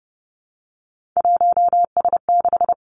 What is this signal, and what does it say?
Morse code, heard as a keyed tone, about 30 wpm, 700 Hz. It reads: 1S6
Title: Three Challenging Letter-Number Combinations 30wpm